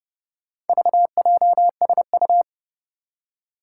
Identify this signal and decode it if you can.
Morse code, heard as a keyed tone, about 30 wpm, 700 Hz. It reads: VJSU
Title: Four Challenging Letter-Number Combinations 30wpm